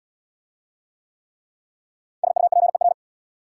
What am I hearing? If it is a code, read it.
Morse code, heard as a keyed tone, about 66 wpm, 700 Hz. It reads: SUPER